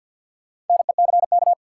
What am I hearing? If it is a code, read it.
Morse code, heard as a keyed tone, about 53 wpm, 700 Hz. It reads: NECK
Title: Sets of 5 Words Encore - New Word Drill - 50wpm